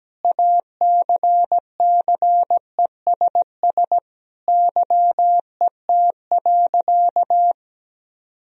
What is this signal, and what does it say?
Morse code, heard as a keyed tone, about 17 wpm, 700 Hz. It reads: ACCESS YET.
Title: Sentences from Top 1000 Words - Review Part 4 - 17wpm